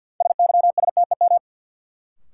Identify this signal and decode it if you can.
Morse code, heard as a keyed tone, about 52 wpm, 700 Hz. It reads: SYSTEM
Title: Sentences from Top 1000 Words - Review Part 3 - New Word Drill - 50wpm